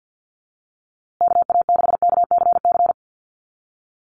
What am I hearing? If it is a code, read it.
Morse code, heard as a keyed tone, about 50 wpm, 700 Hz. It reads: KA6KCC
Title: USA Call Signs Repeated 2x 50wpm